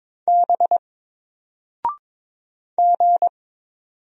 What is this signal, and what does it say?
Morse code, heard as a keyed tone, about 22 wpm, 700 Hz. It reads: B G
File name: Single-Letter-Number-22wpm